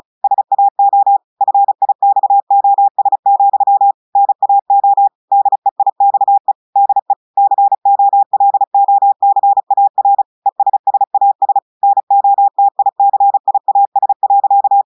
Morse code, heard as a keyed tone, about 35 wpm, 800 Hz. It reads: SAO FIXOS, NAO DEIXE DE COLOCAR ESSAS NOTICIAS.
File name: audios / portugais / portugais texte en clair 2 vitesse 35 mots minutes